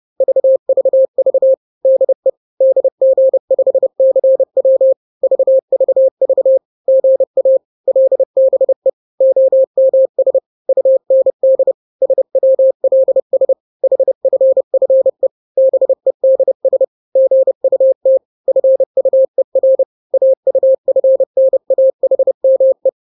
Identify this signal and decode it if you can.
Morse code, heard as a keyed tone, about 30 wpm, 530 Hz. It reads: VVV DE DG5CW VVV GA LBE OMS UND SWLS HFFE BEDS GUT FUER AUFNAHME
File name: daily / 29wpm